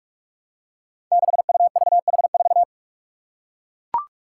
Morse code, heard as a keyed tone, about 42 wpm, 700 Hz. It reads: 6UVH4
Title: Five Challenging Letter-Number Combinations 45wpm